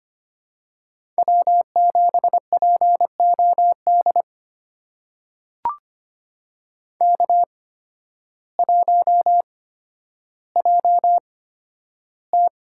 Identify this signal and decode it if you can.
Morse code, heard as a keyed tone, about 25 wpm, spaced as for 12 wpm, 700 Hz. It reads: W7POD K1JT
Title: USA Call Signs - 8x Character Spacing - 25wpm